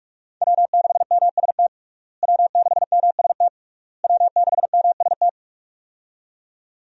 Morse code, heard as a keyed tone, about 44 wpm, 700 Hz. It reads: W6MST W6MST W6MST
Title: USA Call Signs - 5 Characters - Repeated 3x - 45wpm